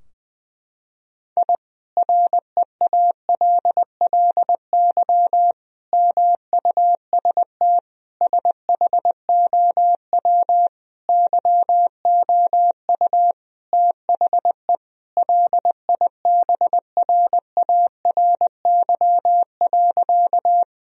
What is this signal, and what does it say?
Morse code, heard as a keyed tone, about 21 wpm, 700 Hz. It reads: I REALLY MUST SHOW YOU THE LIBRARY.
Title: Sentences from Top 1000 Words - Review Part 4 - 20wpm